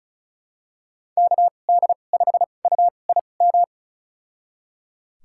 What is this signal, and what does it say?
Morse code, heard as a keyed tone, about 36 wpm, spaced as for 25 wpm, 690 Hz. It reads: KD5UIM
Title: USA Call Signs - 6 Characters - 2x Character Spacing - 35wpm